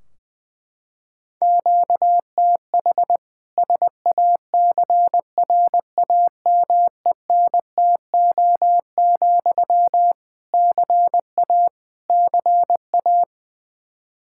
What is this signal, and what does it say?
Morse code, heard as a keyed tone, about 21 wpm, 700 Hz. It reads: QTH SACRAMENTO, CA CA
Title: K1USN SST QSO Practice 20wpm